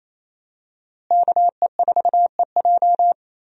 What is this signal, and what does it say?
Morse code, heard as a keyed tone, about 28 wpm, 700 Hz. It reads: KE4EJ
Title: USA Call Signs - 5 Characters - Repeated 3x - 28wpm